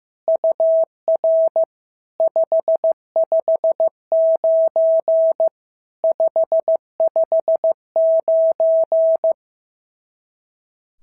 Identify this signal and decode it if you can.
Morse code, heard as a keyed tone, about 14 wpm, 650 Hz. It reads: UR 559 559
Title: SOTA QSO Practice 15wpm